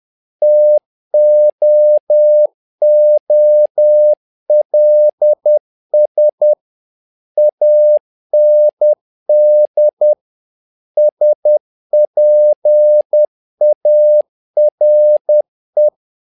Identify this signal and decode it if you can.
Morse code, heard as a keyed tone, about 10 wpm, 600 Hz. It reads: TOOLS AND SPARE
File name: learning center / tower trouble / tower trouble 10wpm